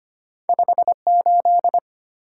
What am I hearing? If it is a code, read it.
Morse code, heard as a keyed tone, about 33 wpm, 700 Hz. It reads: 58